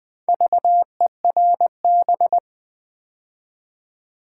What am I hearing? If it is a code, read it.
Morse code, heard as a keyed tone, about 20 wpm, 700 Hz. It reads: VERB
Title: Build A Word - 2 Letters At A Time - 6-Letter Words - 20wpm